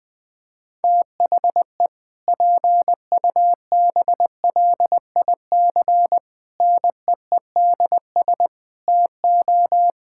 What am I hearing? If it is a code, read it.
Morse code, heard as a keyed tone, about 20 wpm, 700 Hz. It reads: THE PUBLIC NEEDS TO
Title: Sentences with 2-Word Phrases - Spoken Beginning - 20wpm